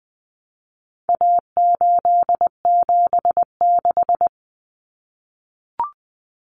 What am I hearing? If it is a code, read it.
Morse code, heard as a keyed tone, about 20 wpm, 700 Hz. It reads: A876
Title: Common Cut Numbers - 4-Digit 20wpm